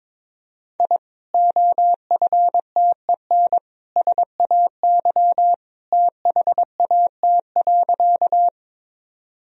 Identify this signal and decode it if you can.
Morse code, heard as a keyed tone, about 22 wpm, 700 Hz. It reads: I OFTEN SAY THAT.
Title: Sentences from Top 1000 Words - Review Part 1 - 22wpm